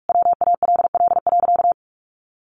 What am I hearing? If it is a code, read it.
Morse code, heard as a keyed tone, about 44 wpm, 700 Hz. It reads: WALL.
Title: Sentences from Top 1100 Words 45wpm